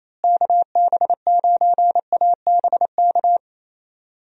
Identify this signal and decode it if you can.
Morse code, heard as a keyed tone, about 27 wpm, 700 Hz. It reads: KB9ABK
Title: USA Call Signs - 6 Characters - Repeated 3x - 28wpm